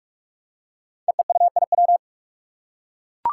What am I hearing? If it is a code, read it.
Morse code, heard as a keyed tone, about 44 wpm, 700 Hz. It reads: EEUIW
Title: Five Challenging Letter-Number Combinations 45wpm